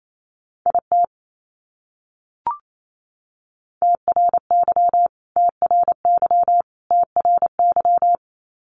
Morse code, heard as a keyed tone, about 28 wpm, 700 Hz. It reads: IT TRY TRY TRY
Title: Top 200 Words Repeated 28wpm